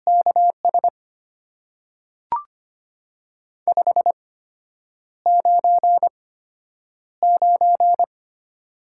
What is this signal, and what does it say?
Morse code, heard as a keyed tone, about 25 wpm, spaced as for 9 wpm, 690 Hz. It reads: KS 599
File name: Top 90 QSO Elements - 8x Character Spacing - 25wpm